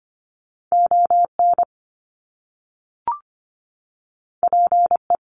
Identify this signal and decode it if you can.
Morse code, heard as a keyed tone, about 25 wpm, 700 Hz. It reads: ON PE
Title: Canadian Provinces and Territories - Repeated 3x - 25wpm